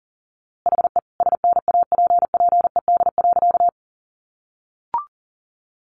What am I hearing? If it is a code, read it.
Morse code, heard as a keyed tone, about 40 wpm, 700 Hz. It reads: HE SNAPPED.
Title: A Sound Of Thunder 40wpm